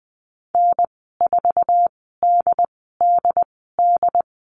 Morse code, heard as a keyed tone, about 20 wpm, spaced as for 14 wpm, 700 Hz. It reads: N4DDD
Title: 076 Lesson 13g - Letter L - Call Signs 20wpm